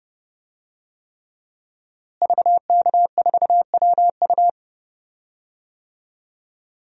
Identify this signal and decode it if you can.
Morse code, heard as a keyed tone, about 30 wpm, 700 Hz. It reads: VK4WU
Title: DXCC-100 Call Signs - 5 Characters - Repeated 3x - 30wpm